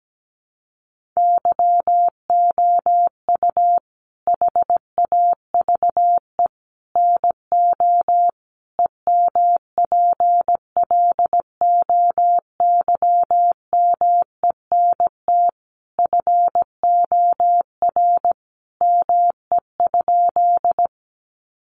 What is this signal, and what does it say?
Morse code, heard as a keyed tone, about 17 wpm, 700 Hz. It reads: YOU HAVE NO EMPLOYMENT FOR ME?
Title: Sentences from Top 1500 Words 17wpm